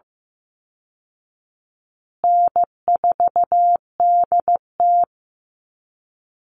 Morse code, heard as a keyed tone, about 15 wpm, 700 Hz. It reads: N4DT
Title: Call Signs Repeated 2x - 15wpm